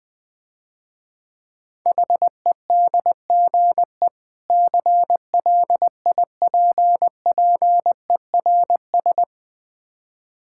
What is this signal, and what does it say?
Morse code, heard as a keyed tone, about 20 wpm, 700 Hz. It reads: HEDGE CLIPPERS
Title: Difficult Words - Speed-Racing with Hints - 20wpm